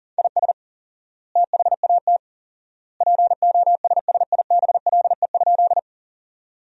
Morse code, heard as a keyed tone, about 40 wpm, 700 Hz. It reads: IS THAT POSSIBLE?
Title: 4x Word-Spacing - Sentences from Top 500 Words - 40wpm